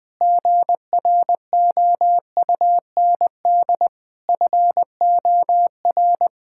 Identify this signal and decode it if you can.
Morse code, heard as a keyed tone, about 20 wpm, 700 Hz. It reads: GROUND FOR
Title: Sentences from Top 1600 Words 20wpm